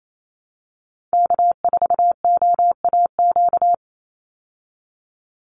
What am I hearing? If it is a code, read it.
Morse code, heard as a keyed tone, about 28 wpm, 700 Hz. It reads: K4OAQ
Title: Call Signs Repeated 3x - 28wpm